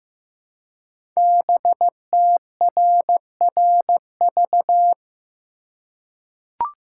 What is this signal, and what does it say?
Morse code, heard as a keyed tone, about 15 wpm, 700 Hz. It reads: BTRRV
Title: Code Groups - 5 Letters - 15wpm